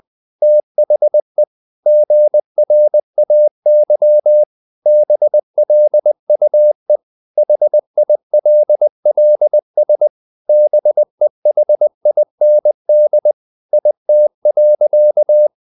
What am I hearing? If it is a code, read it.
Morse code, heard as a keyed tone, about 20 wpm, 600 Hz. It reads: THE GRAY BLUE HILLS BEHIND IT.
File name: the old man and the sea / the old man and the sea part 1 20wpm